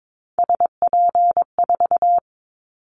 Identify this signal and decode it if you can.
Morse code, heard as a keyed tone, about 22 wpm, 700 Hz. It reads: SP4
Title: Callsign Prefixes for Top 25 DXCC - 22wpm